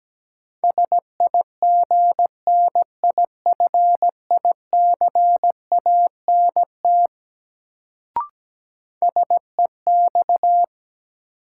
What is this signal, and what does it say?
Morse code, heard as a keyed tone, about 17 wpm, 700 Hz. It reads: SIGNIFICANT SEX
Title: Sentences from Top 1600 Words - New Word Drill Alt - 17wpm